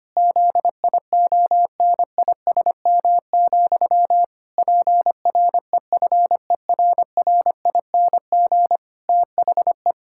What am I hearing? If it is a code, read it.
Morse code, heard as a keyed tone, about 25 wpm, 700 Hz. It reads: ZIONISM, PREFERRING THE
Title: Pangram Challenge v2 - 25wpm